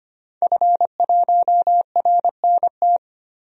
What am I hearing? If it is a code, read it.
Morse code, heard as a keyed tone, about 25 wpm, 700 Hz. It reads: F1RNT
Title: DXCC-100 Call Signs - 5 Characters - 25wpm